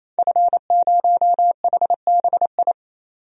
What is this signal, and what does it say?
Morse code, heard as a keyed tone, about 28 wpm, 700 Hz. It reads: F0HBI
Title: DXCC-100 Call Signs - 5 Characters - 28wpm